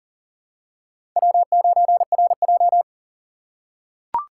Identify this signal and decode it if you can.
Morse code, heard as a keyed tone, about 40 wpm, 700 Hz. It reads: W9RJ
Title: USA Call Signs - 4 Characters - 3x Character Spacing - 40wpm